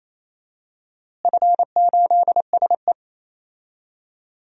Morse code, heard as a keyed tone, about 27 wpm, 700 Hz. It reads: F8SE
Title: DXCC-100 Call Signs - 4 Characters - Repeated 2x - 28wpm